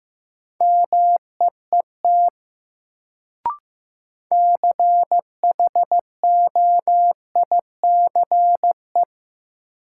Morse code, heard as a keyed tone, about 15 wpm, 700 Hz. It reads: MEET CHOICE
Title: Sets of 5 Words - New Word Drill - 15wpm